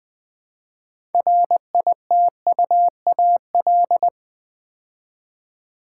Morse code, heard as a keyed tone, about 20 wpm, 700 Hz. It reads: RITUAL
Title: Build A Word - 2 Letters At A Time - 6-Letter Words - 20wpm